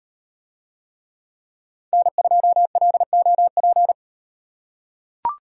Morse code, heard as a keyed tone, about 35 wpm, 700 Hz. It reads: N2LOP